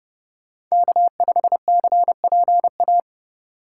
Morse code, heard as a keyed tone, about 36 wpm, 700 Hz. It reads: K5CPA
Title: Speed Racing - USA Call Signs 30wpm